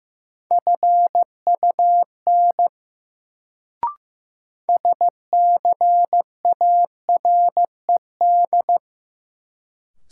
Morse code, heard as a keyed tone, about 16 wpm, 700 Hz. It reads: FUN SCARED
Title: Sentences from Top 700 Words - New Word Drill - 15wpm